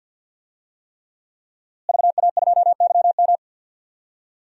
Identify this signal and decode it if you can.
Morse code, heard as a keyed tone, about 53 wpm, 700 Hz. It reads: VA2YM
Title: DXCC-100 Call Signs - 5 Characters - Repeated 3x - 50wpm